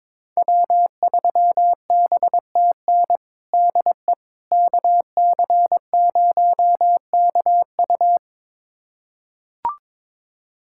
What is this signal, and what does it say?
Morse code, heard as a keyed tone, about 22 wpm, 700 Hz. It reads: W3BTN DE KC0KU
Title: USA Callsign DE Callsign - 5 Characters - 22wpm